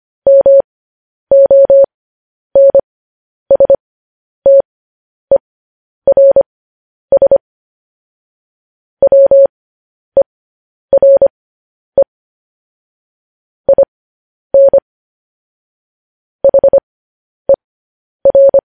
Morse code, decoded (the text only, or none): MONSTERS WERE IN HER